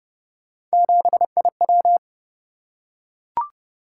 Morse code, heard as a keyed tone, about 28 wpm, 700 Hz. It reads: 7IW